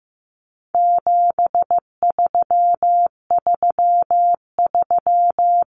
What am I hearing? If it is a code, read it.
Morse code, heard as a keyed tone, about 15 wpm, 700 Hz. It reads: 7333
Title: Numbers - 4-Digit 15wpm